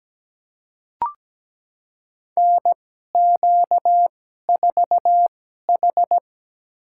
Morse code, heard as a keyed tone, about 17 wpm, spaced as for 12 wpm, 700 Hz. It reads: NQ4H
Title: USA Call Signs - 4 Characters - 2x Character Spacing - 17wpm